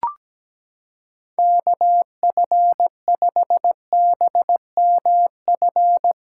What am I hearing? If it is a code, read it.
Morse code, heard as a keyed tone, about 17 wpm, 700 Hz. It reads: KF5BMF
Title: USA Callsign DE Callsign - 6 Characters - 17wpm